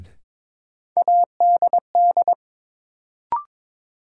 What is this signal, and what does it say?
Morse code, heard as a keyed tone, about 21 wpm, 700 Hz. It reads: ADD
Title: Sentences from Top 1000 Words - Review Part 1 - New Word Drill - 22wpm